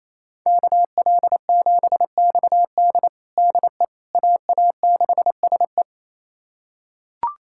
Morse code, heard as a keyed tone, about 28 wpm, 700 Hz. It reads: KL7XD DE AA6SE
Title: USA Callsign DE Callsign - 5 Characters - 28wpm